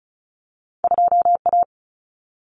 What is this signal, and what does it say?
Morse code, heard as a keyed tone, about 36 wpm, 700 Hz. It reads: 2A
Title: Common Cut Numbers - 2-Digit 35wpm